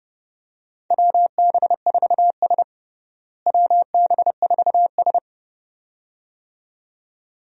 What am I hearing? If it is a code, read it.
Morse code, heard as a keyed tone, about 29 wpm, 700 Hz. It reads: WB4S WB4S
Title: USA Call Signs - 4 Characters - Repeated 2x - 30wpm